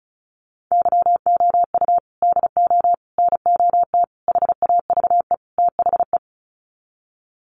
Morse code, heard as a keyed tone, about 34 wpm, 700 Hz. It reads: YOU DO NOT HAVE THE
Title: Sets of 5 Words 35wpm